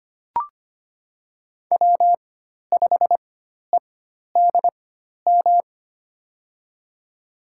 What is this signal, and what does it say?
Morse code, heard as a keyed tone, about 25 wpm, spaced as for 12 wpm, 700 Hz. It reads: W5EDM
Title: USA Call Signs - 5 Characters - 4x Character Spacing - 25wpm